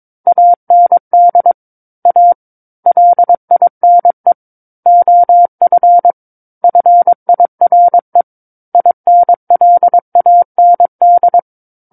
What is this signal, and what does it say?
Morse code, heard as a keyed tone, about 22 wpm, 700 Hz. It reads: AND A LINE OF FIRE INLAND
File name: Book One / 13-Chapter 13 - 22 WPM